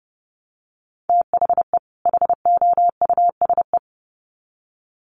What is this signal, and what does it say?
Morse code, heard as a keyed tone, about 28 wpm, 700 Hz. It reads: THE HOUSE
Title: Sets of 2 Words 30wpm